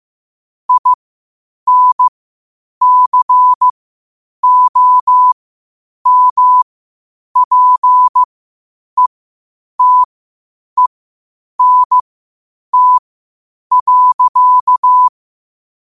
Morse code, heard as a keyed tone, about 15 wpm, spaced as for 8 wpm, 1000 Hz. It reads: INCOMPETENT.
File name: qotdfiles / 8wpm